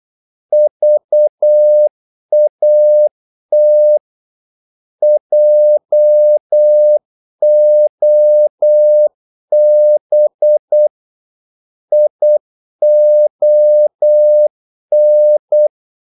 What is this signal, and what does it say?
Morse code, heard as a keyed tone, about 8 wpm, 600 Hz. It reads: VAT JOB ION
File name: Korte ord DK 8wpm